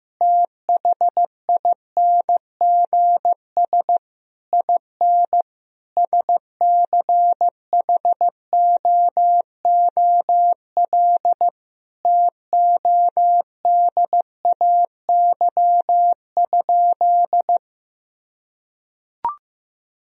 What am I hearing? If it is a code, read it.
Morse code, heard as a keyed tone, about 14 wpm, 700 Hz. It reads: THINGS IN SCHOOL TODAY?
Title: Sentences from Top 1000 Words - Review Part 4 - 15wpm